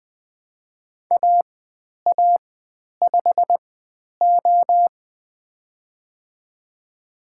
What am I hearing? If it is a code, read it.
Morse code, heard as a keyed tone, about 20 wpm, spaced as for 10 wpm, 700 Hz. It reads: AA5O
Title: 095 Lesson 16g - Letter c - Call Signs 20wpm